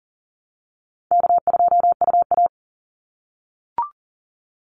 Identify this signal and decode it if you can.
Morse code, heard as a keyed tone, about 40 wpm, 700 Hz. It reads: K2UA
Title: Easy Call Signs - 40wpm